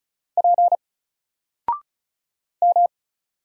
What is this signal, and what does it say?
Morse code, heard as a keyed tone, about 33 wpm, 700 Hz. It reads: P M
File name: Single-Letter-Number-35wpm